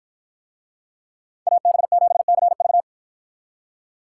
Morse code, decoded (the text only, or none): AB7CV